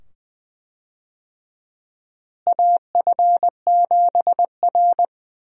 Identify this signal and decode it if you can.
Morse code, heard as a keyed tone, about 20 wpm, 700 Hz. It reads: AF7R
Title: USA Call Signs - 4 Characters - Repeated 2x - 20wpm